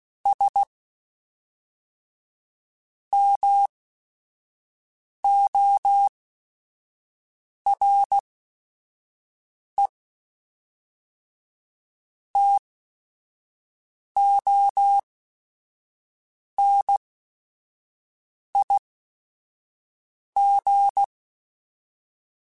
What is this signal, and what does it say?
Morse code, heard as a keyed tone, about 16 wpm, spaced as for 5 wpm, 800 Hz. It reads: S MORE TONIG
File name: QSOs / QSOs-mp3s / WA7QOX 1@5